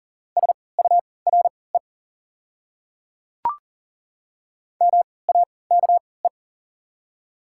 Morse code, heard as a keyed tone, about 40 wpm, spaced as for 23 wpm, 700 Hz. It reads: SURE MAKE
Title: 3x Character Spacing - Top 300 Words - 40wpm